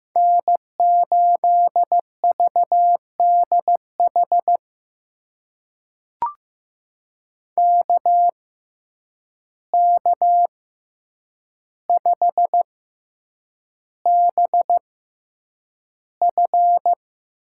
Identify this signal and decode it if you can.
Morse code, heard as a keyed tone, about 15 wpm, spaced as for 8 wpm, 700 Hz. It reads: N8VDH KK5BF
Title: USA Call Signs - 6x Character Spacing - 15wpm